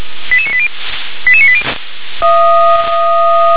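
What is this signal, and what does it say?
Five-tone selective call (ZVEI), heard as quick runs of five tones, tones of about 70 ms. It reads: 88980 88980
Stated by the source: ZVEI five-tone selective call received on an SDR.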